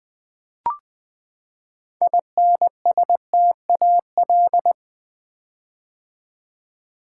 Morse code, heard as a keyed tone, about 20 wpm, 700 Hz. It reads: INSTAL
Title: Build A Word - 6 Letters At A Time - 12-Letter Words - 20wpm